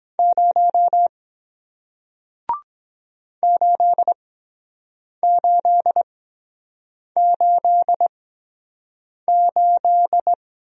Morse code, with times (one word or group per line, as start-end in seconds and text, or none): 0.2-1.1 0
3.4-10.3 8888